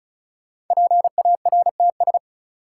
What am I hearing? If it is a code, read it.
Morse code, heard as a keyed tone, about 36 wpm, 700 Hz. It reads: PARTS
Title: Sets of 3 Words - New Word Drill - 35wpm